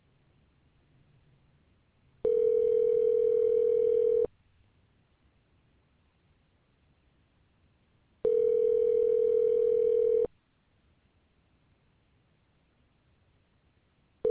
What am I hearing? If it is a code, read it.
Ringback tone (North America), 440 + 480 Hz.